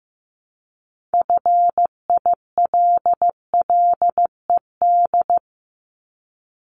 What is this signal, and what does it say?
Morse code, heard as a keyed tone, about 15 wpm, 700 Hz. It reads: FILLED
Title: ED Suffix - Top 100 Words with Base Words 15wpm